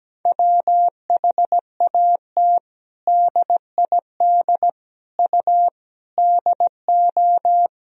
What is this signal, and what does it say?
Morse code, heard as a keyed tone, about 17 wpm, 700 Hz. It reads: WHAT DID U DO
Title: Sentences from Top 100 Words and Abbreviations 17wpm